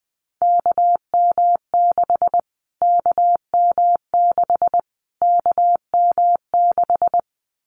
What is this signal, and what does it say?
Morse code, heard as a keyed tone, about 20 wpm, 700 Hz. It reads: KM6 KM6 KM6
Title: Callsign Prefixes for Top 25 DXCC - Repeated 3x - 20wpm